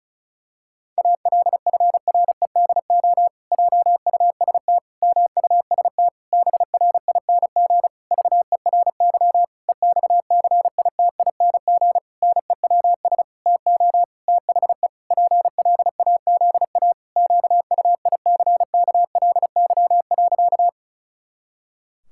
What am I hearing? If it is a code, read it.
Morse code, heard as a keyed tone, about 36 wpm, 700 Hz. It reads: ALFREDO JUST MUST BRING VERY EXCITING NEWS TO THE PLAZA QUICKLY.
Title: Pangram Challenge v2 - 35wpm